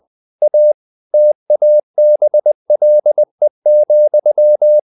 Morse code, heard as a keyed tone, about 20 wpm, 600 Hz. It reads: A TABLE,
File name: the old man and the sea / the old man and the sea part 1 20wpm